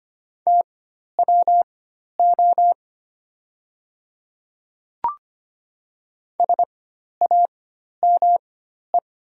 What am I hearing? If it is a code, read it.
Morse code, heard as a keyed tone, about 25 wpm, spaced as for 12 wpm, 690 Hz. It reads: TWO SAME
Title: 4x Character Spacing - Top 300 Words - 25wpm